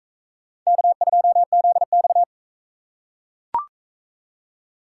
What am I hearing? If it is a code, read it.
Morse code, heard as a keyed tone, about 39 wpm, 700 Hz. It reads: K2ZX